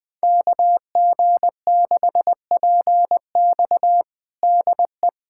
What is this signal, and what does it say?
Morse code, heard as a keyed tone, about 20 wpm, 700 Hz. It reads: KG6PX DE
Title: USA Callsign DE Callsign - 5 Characters - 20wpm